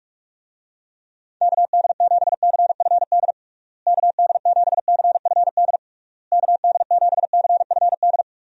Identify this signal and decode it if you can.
Morse code, heard as a keyed tone, about 46 wpm, 700 Hz. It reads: KD7CFD KD7CFD KD7CFD
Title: USA Call Signs - 6 Characters - Repeated 3x - 45wpm